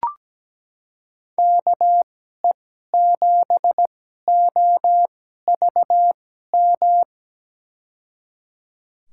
Morse code, heard as a keyed tone, about 17 wpm, spaced as for 12 wpm, 700 Hz. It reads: KE7OVM
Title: USA Call Signs - 6 Characters - 2x Character Spacing - 17wpm